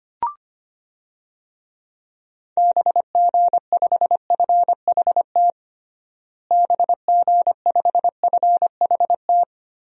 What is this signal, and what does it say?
Morse code, heard as a keyed tone, about 25 wpm, 700 Hz. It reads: BG5FHT BG5FHT
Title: DXCC-100 Call Signs - 6 Characters - Repeated 3x - 25wpm